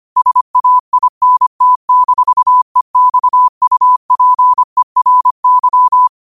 Morse code, heard as a keyed tone, about 25 wpm, 1000 Hz. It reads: SAINT-EXUPERY